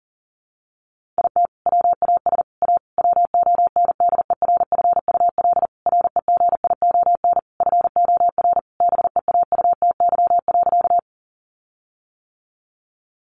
Morse code, heard as a keyed tone, about 40 wpm, 700 Hz. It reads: IT WAS A WONDERFUL REGION FOR BEAUTY.
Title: Sentences from Top 1500 Words 40wpm